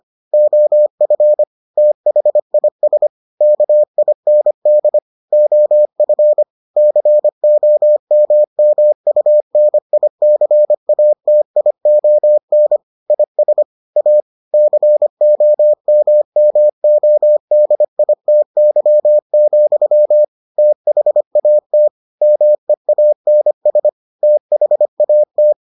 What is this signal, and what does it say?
Morse code, heard as a keyed tone, about 25 wpm, 600 Hz. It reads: OF THIS KIND OF COMMUNICATION IS A COMMODITY, THAT MEANS THAT